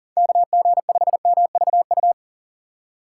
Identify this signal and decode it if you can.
Morse code, heard as a keyed tone, about 43 wpm, 700 Hz. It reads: KG5MVU
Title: USA Call Signs - 6 Characters - 6x Character Spacing - 40wpm